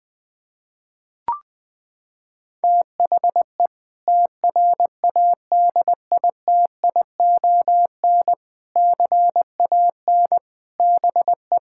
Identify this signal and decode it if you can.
Morse code, heard as a keyed tone, about 20 wpm, 700 Hz. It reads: THE TRADITION CAN BE
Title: Sentences with 6-Word Phrases - 20wpm